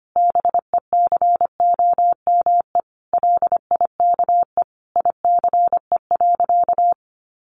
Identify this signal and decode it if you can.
Morse code, heard as a keyed tone, about 25 wpm, 700 Hz. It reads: BECOME LIKE ICE.
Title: Sentences from Top 1000 Words 25wpm